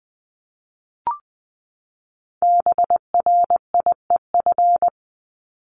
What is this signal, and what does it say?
Morse code, heard as a keyed tone, about 20 wpm, 700 Hz. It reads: BRIEF
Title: LY Suffix - Top 100 Words with Base Words 20wpm